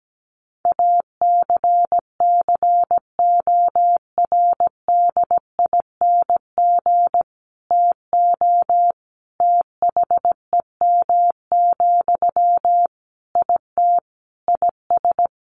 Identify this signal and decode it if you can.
Morse code, heard as a keyed tone, about 17 wpm, 700 Hz. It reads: ACCORDING TO THEM, IT IS
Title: Sentences from Top 800 Words 17wpm